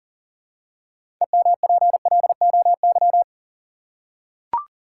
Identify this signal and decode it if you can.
Morse code, heard as a keyed tone, about 40 wpm, 700 Hz. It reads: EMPLOY
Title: Six Letter Words - 40wpm